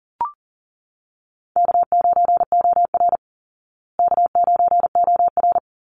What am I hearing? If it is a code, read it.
Morse code, heard as a keyed tone, about 40 wpm, 700 Hz. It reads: K9OR K9OR
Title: Easy Call Signs Repeated 3x - 40wpm